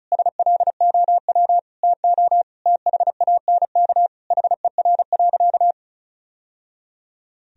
Morse code, heard as a keyed tone, about 34 wpm, 700 Hz. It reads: SLOW TO THANK HER.
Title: Sentences from Top 1000 Words - Review Part 2 - 35wpm